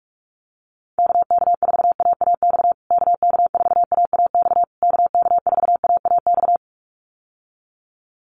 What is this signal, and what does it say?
Morse code, heard as a keyed tone, about 46 wpm, 700 Hz. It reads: KK4AAX KK4AAX KK4AAX
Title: USA Call Signs Repeated 3x 45wpm